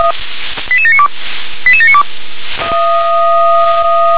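Five-tone selective call (ZVEI), heard as quick runs of five tones, tones of about 70 ms. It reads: 88792 88792
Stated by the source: ZVEI five-tone selective call received by SDR.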